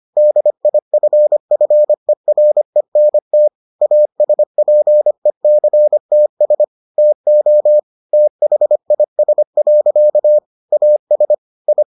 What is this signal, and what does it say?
Morse code, heard as a keyed tone, about 25 wpm, 600 Hz. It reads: DIFFERENT ASPECTS TO THIS. AS I